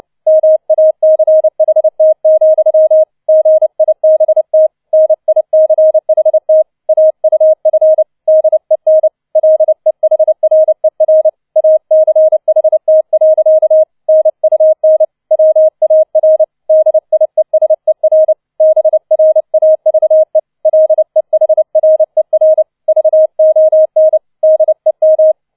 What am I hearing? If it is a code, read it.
Morse code, heard as a keyed tone, about 29 wpm, 630 Hz. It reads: MACHT, GIBT NICHT AUF DEN LEHRER ACHT. NUN WAR DIESER BRAVE LEHRER VON DEM
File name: 004 Vierter Streich 30wpm